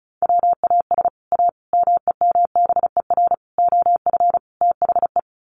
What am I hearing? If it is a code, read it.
Morse code, heard as a keyed tone, about 34 wpm, 700 Hz. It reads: WAS A MEMBER OF THE
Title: Sets of 5 Words 35wpm